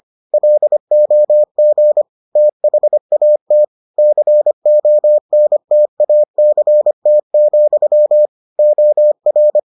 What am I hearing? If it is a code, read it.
Morse code, heard as a keyed tone, about 25 wpm, 600 Hz. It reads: LOG THAT CONTACT, OR